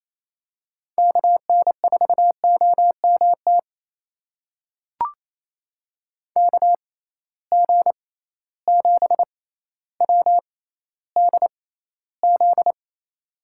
Morse code, heard as a keyed tone, about 27 wpm, spaced as for 14 wpm, 700 Hz. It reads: KN4OMT KG7WDZ
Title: USA Call Signs - 6 Characters - 6x Character Spacing - 28wpm